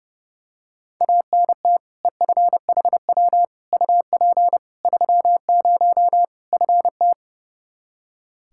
Morse code, heard as a keyed tone, about 30 wpm, 700 Hz. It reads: ANT EFHW UP 30 FT
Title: Practice QSO - Exchange 2 - Info - 30wpm